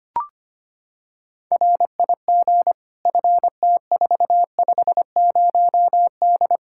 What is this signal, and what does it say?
Morse code, heard as a keyed tone, about 25 wpm, 700 Hz. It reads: RIG FT450D
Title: Practice QSO - Exchange 2 - Info - 25wpm